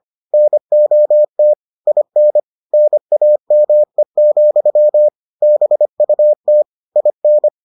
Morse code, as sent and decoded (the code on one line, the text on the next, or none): -. --- - / .. -. / -. .- -- . --..-- / -... ..- - / .. -.
NOT IN NAME, BUT IN